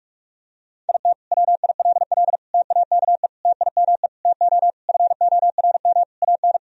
Morse code, heard as a keyed tone, about 42 wpm, 700 Hz. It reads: IT WILL TAKE TIME TO FORM AN
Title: Sentences from Top 1000 Words - Review Part 1 - 45wpm